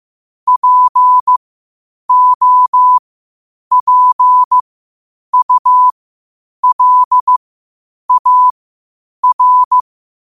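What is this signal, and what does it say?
Morse code, heard as a keyed tone, about 15 wpm, spaced as for 8 wpm, 1000 Hz. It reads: POPULAR
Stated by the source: Morse code quotes at 8 words per minute from QOTD at 8 WPM.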